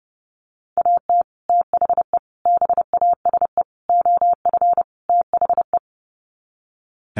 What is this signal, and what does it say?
Morse code, heard as a keyed tone, about 30 wpm, 700 Hz. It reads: AT THE BASE OF THE
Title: Sets of 5 Words 30wpm